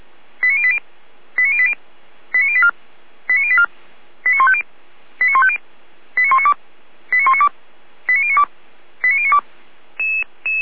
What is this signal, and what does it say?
Five-tone selective call (ZVEI), heard as quick runs of five tones, tones of about 70 ms. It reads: 89080 89080 89084 89084 89150 89150 89192 89192 89092 89092
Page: /455/SDR/ZVEI/89110/